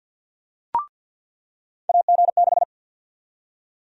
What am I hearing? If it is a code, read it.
Morse code, heard as a keyed tone, about 53 wpm, 700 Hz. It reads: AG6
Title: USA Call Sign Prefixes - 50wpm